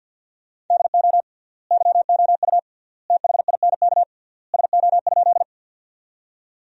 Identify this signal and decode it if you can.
Morse code, heard as a keyed tone, about 50 wpm, 700 Hz. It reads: DO YOU THINK SO?
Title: 3x Word-Spacing - Sentences from Top 200 Words - 50wpm